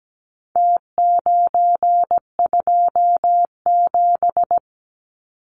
Morse code, heard as a keyed tone, about 17 wpm, 700 Hz. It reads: T927
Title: Numbers - Contest Style - 4-Digit 17wpm